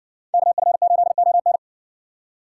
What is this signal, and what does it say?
Morse code, heard as a keyed tone, about 56 wpm, 700 Hz. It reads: KF8ON